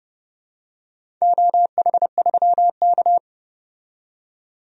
Morse code, heard as a keyed tone, about 30 wpm, 700 Hz. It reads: OH3K
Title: DXCC-100 Call Signs - 4 Characters - 30wpm